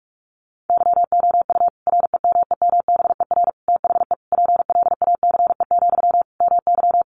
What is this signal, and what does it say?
Morse code, heard as a keyed tone, about 46 wpm, 700 Hz. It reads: YOU REMEMBER THE PLACE, MY
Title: Sentences from Top 400 Words 45wpm